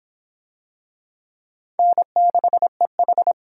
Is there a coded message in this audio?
Yes: Morse code from 1.8 to 3.3 s, heard as a keyed tone, about 24 wpm, 700 Hz. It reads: N6EH